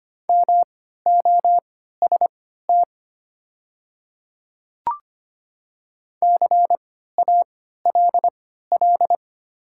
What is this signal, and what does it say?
Morse code, heard as a keyed tone, about 25 wpm, spaced as for 14 wpm, 700 Hz. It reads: MOST CALL
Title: 3x Character Spacing - Top 100 Words - 25wpm